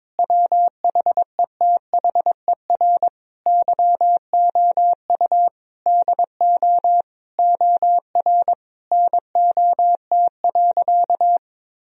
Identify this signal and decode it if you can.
Morse code, heard as a keyed tone, about 22 wpm, 700 Hz. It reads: WHETHER YOU DO OR NOT.
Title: Sentences from Top 1000 Words - Review Part 3 - 22wpm